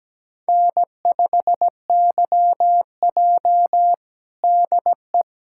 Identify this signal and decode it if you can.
Morse code, heard as a keyed tone, about 17 wpm, 700 Hz. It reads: N5YJ DE
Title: USA Callsign DE Callsign - 4 Characters - 17wpm